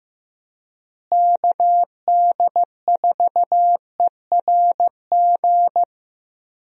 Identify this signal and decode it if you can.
Morse code, heard as a keyed tone, about 15 wpm, 700 Hz. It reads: KD4ERG
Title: USA Call Signs - 6 Characters - Repeated 2x - 15wpm